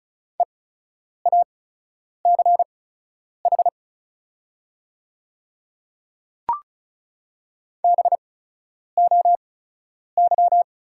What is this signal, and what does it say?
Morse code, heard as a keyed tone, about 36 wpm, spaced as for 10 wpm, 690 Hz. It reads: EACH BOY
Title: 8x Character Spacing - Top 300 Words - 35wpm